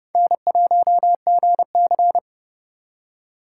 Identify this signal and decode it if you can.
Morse code, heard as a keyed tone, about 28 wpm, 700 Hz. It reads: N1GC